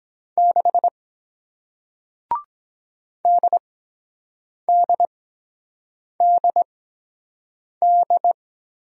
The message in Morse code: 6 DDDD

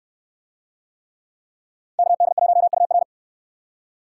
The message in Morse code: KD1UR

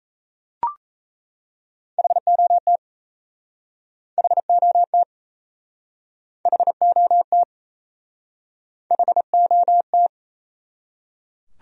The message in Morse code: HOT HOT HOT HOT

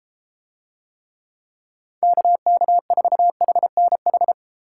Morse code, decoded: KK4HNH

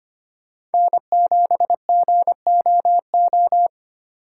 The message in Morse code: N7GOO

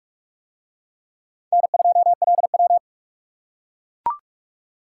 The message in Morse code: N2LW

